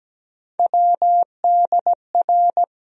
Morse code, heard as a keyed tone, about 17 wpm, 700 Hz. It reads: WDR
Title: USA Call Sign Postfixes - 3 Letters 17wpm